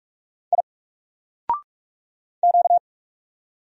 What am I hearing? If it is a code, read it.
Morse code, heard as a keyed tone, about 44 wpm, 700 Hz. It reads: I Q
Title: Single Letter-Number 45wpm